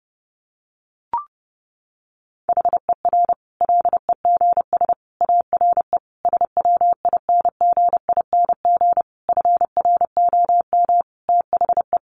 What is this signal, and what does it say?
Morse code, heard as a keyed tone, about 30 wpm, 700 Hz. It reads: HER LEGS ARE SWINGING FROM THE
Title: Easy Phrases 30wpm